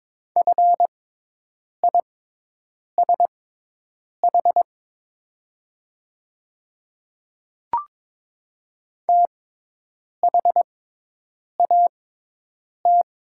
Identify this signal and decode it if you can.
Morse code, heard as a keyed tone, about 22 wpm, spaced as for 8 wpm, 700 Hz. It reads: FISH THAT
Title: 6x Character Spacing - Top 500 Words - 22wpm